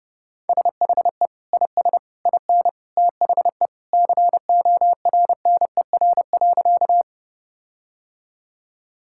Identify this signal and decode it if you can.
Morse code, heard as a keyed tone, about 32 wpm, 700 Hz. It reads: SHE IS IN THE CORNER.
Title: Sentences from Top 1600 Words 30wpm